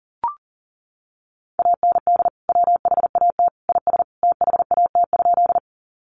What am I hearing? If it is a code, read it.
Morse code, heard as a keyed tone, about 40 wpm, 700 Hz. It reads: AND WHAT IS THAT?
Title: Sentences from Top 100 Words 40wpm